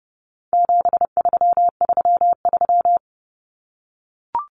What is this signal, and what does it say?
Morse code, heard as a keyed tone, about 30 wpm, 700 Hz. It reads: 7333
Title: Numbers - 4-Digit 30wpm